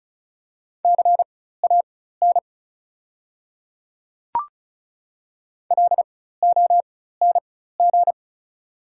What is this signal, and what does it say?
Morse code, heard as a keyed tone, about 36 wpm, spaced as for 16 wpm, 690 Hz. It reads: CAN LONG
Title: 4x Character Spacing - Top 100 Words - 35wpm